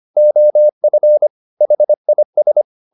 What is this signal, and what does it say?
Morse code, heard as a keyed tone, about 25 wpm, 600 Hz. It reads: OF HIS
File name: the old man and the sea part 1 25wpm